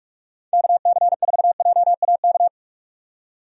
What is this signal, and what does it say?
Morse code, heard as a keyed tone, about 41 wpm, 700 Hz. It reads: KC4JAK
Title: USA Call Signs - 6 Characters - 45wpm